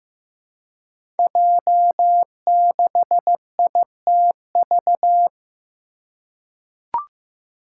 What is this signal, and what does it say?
Morse code, heard as a keyed tone, about 15 wpm, 700 Hz. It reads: J6ITV
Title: Five Challenging Letter-Number Combinations 15wpm